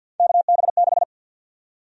Morse code, heard as a keyed tone, about 53 wpm, 700 Hz. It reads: KB6
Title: USA Call Sign Prefixes - Repeated 2x - 50wpm